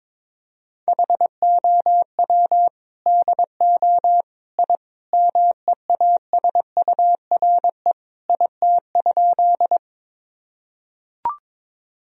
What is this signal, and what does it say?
Morse code, heard as a keyed tone, about 22 wpm, 700 Hz. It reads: HOW DO I MEASURE IT?
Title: Sentences from Top 1000 Words - Review Part 2 - 22wpm